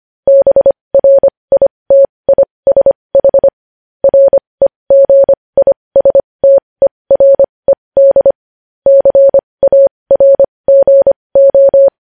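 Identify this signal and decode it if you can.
Morse code, heard as a keyed tone, about 25 wpm, spaced as for 20 wpm, 550 Hz. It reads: BRITISH REGISTERED CARGO
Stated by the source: Morse practice of news headlines at 20 wpm.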